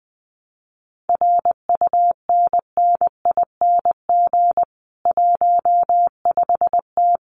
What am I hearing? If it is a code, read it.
Morse code, heard as a keyed tone, about 20 wpm, 700 Hz. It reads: RUNNING 15T
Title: Common QSO Phrases 20wpm